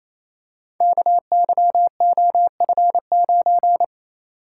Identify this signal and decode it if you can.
Morse code, heard as a keyed tone, about 29 wpm, 700 Hz. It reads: KYOF9
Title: Code Groups - 5 Letter:Numbers - 28wpm